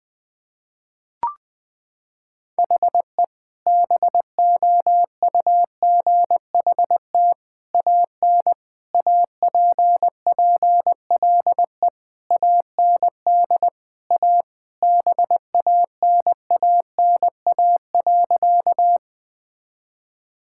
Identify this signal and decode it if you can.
Morse code, heard as a keyed tone, about 21 wpm, 700 Hz. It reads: HE BOUGHT AN APPLE AND A BANANA.
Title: Sentences with 2-Word Phrases - 20wpm